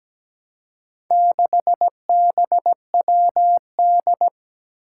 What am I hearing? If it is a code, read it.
Morse code, heard as a keyed tone, about 17 wpm, 700 Hz. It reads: 6BWD
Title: Four Challenging Letter-Number Combinations 17wpm